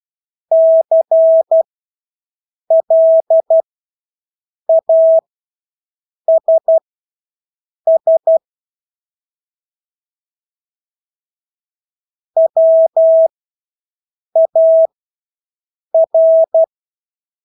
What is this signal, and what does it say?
Morse code, heard as a keyed tone, about 12 wpm, spaced as for 6 wpm, 650 Hz. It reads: CLASS WAR
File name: Engelske ord 12-6 wpm